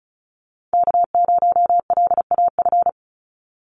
Morse code, heard as a keyed tone, about 34 wpm, 700 Hz. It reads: K0LAF
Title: NA SOTA Chaser Callsigns 35wpm